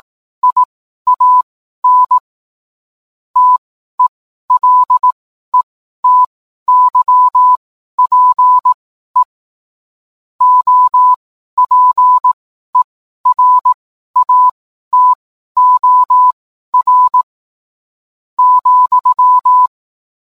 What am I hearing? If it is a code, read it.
Morse code, heard as a keyed tone, about 18 wpm, spaced as for 12 wpm, 1000 Hz. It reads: IAN TELETYPE OPERATOR ,
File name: cwblog / cw mp3 directory / mem morse 12